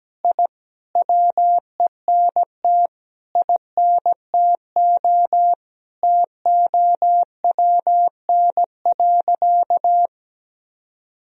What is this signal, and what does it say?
Morse code, heard as a keyed tone, about 17 wpm, 700 Hz. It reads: I WENT INTO TOWN.
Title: Sentences from Top 1000 Words - Review Part 2 - 17wpm